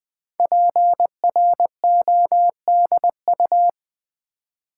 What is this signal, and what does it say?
Morse code, heard as a keyed tone, about 20 wpm, 700 Hz. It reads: PRODU
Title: Build A Word - 5 Letters At A Time - 10-Letter Words - 20wpm